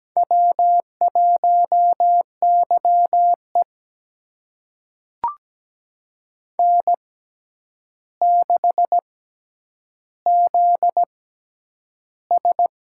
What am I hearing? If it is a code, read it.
Morse code, heard as a keyed tone, about 17 wpm, spaced as for 9 wpm, 700 Hz. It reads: W1YE N6ZS
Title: USA Call Signs - 4 Characters - 6x Character Spacing - 17wpm